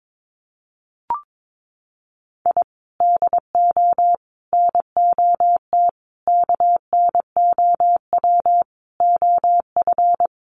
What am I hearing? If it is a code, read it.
Morse code, heard as a keyed tone, about 22 wpm, 700 Hz. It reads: I DO NOT KNOW OF
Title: Sets of 5 Words Encore2 22wpm